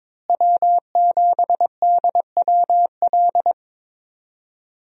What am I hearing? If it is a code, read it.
Morse code, heard as a keyed tone, about 22 wpm, 700 Hz. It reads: W7DWL
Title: USA Call Signs - 5 Characters - Repeated 2x - 22wpm